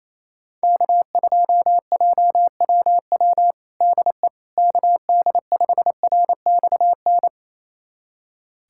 Morse code, heard as a keyed tone, about 28 wpm, 700 Hz. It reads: K2JWW DE KD5RXN
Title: USA Callsign DE Callsign 28wpm